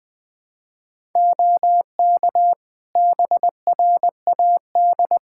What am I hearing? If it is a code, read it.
Morse code, heard as a keyed tone, about 20 wpm, 700 Hz. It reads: OK BRAD
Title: Practice QSO - Exchange 3 - More Info - 20wpm